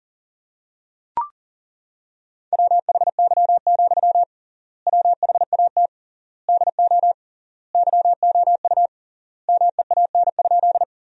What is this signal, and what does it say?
Morse code, heard as a keyed tone, about 40 wpm, 700 Hz. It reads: WHY, WHAT DO YOU MEAN?
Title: 3x Word-Spacing - Sentences from Top 200 Words - 40wpm